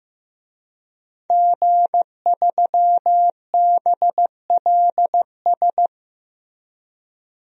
Morse code, heard as a keyed tone, about 15 wpm, 700 Hz. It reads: G3BLS
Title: DXCC-100 Call Signs - 5 Characters - 15wpm